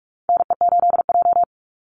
Morse code, heard as a keyed tone, about 44 wpm, 700 Hz. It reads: NE8J
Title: Call Signs Repeated 3x - 45wpm